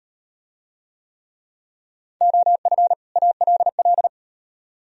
Morse code, heard as a keyed tone, about 38 wpm, 700 Hz. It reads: OF ALL